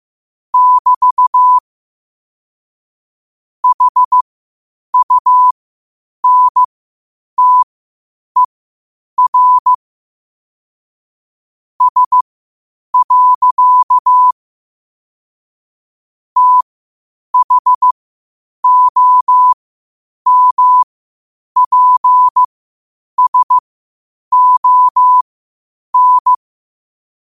Morse code, heard as a keyed tone, about 15 wpm, spaced as for 8 wpm, 1000 Hz. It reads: <BT> HUNTER S. THOMPSON
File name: qotdfiles / 8wpm